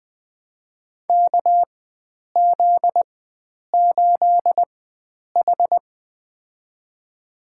Morse code, heard as a keyed tone, about 20 wpm, spaced as for 9 wpm, 700 Hz. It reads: KZ8H
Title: USA Call Signs - 4 Characters - 4x Character Spacing - 20wpm